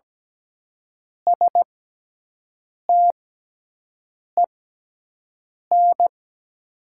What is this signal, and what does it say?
Morse code, heard as a keyed tone, about 17 wpm, spaced as for 6 wpm, 700 Hz. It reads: STEN
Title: 6x Character Spacing - Top 500 Words - 17wpm